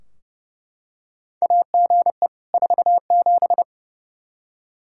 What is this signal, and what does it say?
Morse code, heard as a keyed tone, about 29 wpm, 700 Hz. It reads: AGE 47
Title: Practice QSO - Exchange 3 - More Info - 30wpm